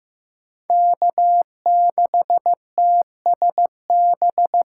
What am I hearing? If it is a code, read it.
Morse code, heard as a keyed tone, about 18 wpm, 700 Hz. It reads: K6TSB
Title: Speed Racing - USA 1x3 Call Signs 15wpm